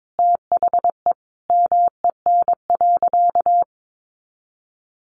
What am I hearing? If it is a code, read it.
Morse code, heard as a keyed tone, about 22 wpm, 700 Hz. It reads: THE MEN.
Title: A Sound Of Thunder 22wpm